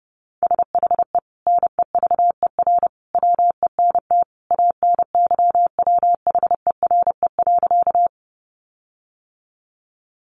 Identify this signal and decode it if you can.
Morse code, heard as a keyed tone, about 30 wpm, 700 Hz. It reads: SHE NEVER WENT ANYWHERE.
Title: Sentences from Top 1400 Words 30wpm